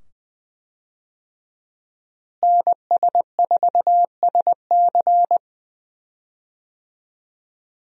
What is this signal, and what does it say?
Morse code, heard as a keyed tone, about 20 wpm, 700 Hz. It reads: NS4SC
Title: USA Call Signs - 5 Characters - Repeated 2x - 20wpm